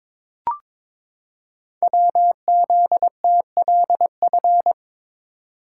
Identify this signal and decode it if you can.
Morse code, heard as a keyed tone, about 22 wpm, 690 Hz. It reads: WZTLF
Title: Code Groups - 5 Letters - 22wpm